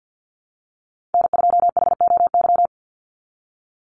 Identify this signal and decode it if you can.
Morse code, heard as a keyed tone, about 50 wpm, 700 Hz. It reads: N2HOY
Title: USA Call Signs 50wpm